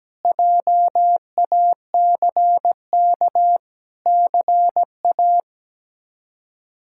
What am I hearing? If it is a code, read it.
Morse code, heard as a keyed tone, about 17 wpm, 700 Hz. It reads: JACK CA
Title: K1USN SST QSO Practice - 17wpm